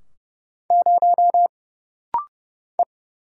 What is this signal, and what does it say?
Morse code, heard as a keyed tone, about 30 wpm, 700 Hz. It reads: 0 E
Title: Single Letter-Number Alt - Rapid-Fire 30wpm